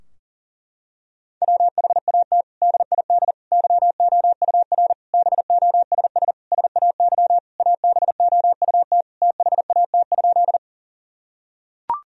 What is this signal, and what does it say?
Morse code, heard as a keyed tone, about 40 wpm, 700 Hz. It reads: WHAT DID YOUR BOSS SAY ABOUT THAT?
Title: Sentences from Top 1000 Words - Review Part 4 - 40wpm